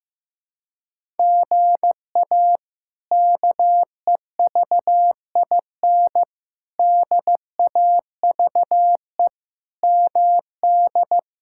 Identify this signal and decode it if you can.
Morse code, heard as a keyed tone, about 15 wpm, 700 Hz. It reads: GA KEVIN DAVE MD
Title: K1USN SST QSO Practice - 15wpm